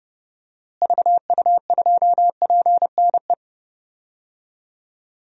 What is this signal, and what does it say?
Morse code, heard as a keyed tone, about 30 wpm, 700 Hz. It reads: VU2PNE
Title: DXCC-100 Call Signs - 6 Characters - Repeated 2x - 30wpm